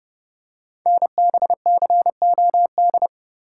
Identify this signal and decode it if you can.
Morse code, heard as a keyed tone, about 30 wpm, 700 Hz. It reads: NBCOD
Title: Code Groups - 5 Letters - 30wpm